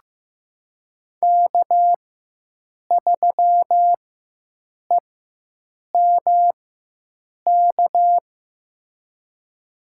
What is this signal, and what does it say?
Morse code, heard as a keyed tone, about 15 wpm, spaced as for 7 wpm, 700 Hz. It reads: K3EMK
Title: USA Call Signs - 5 Characters - 4x Character Spacing - 15wpm